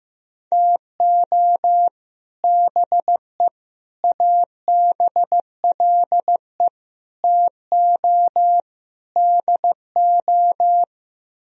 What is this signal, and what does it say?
Morse code, heard as a keyed tone, about 15 wpm, 700 Hz. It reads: TO BE ABLE TO DO
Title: Speed-Racing - Sets of 5 Words 15wpm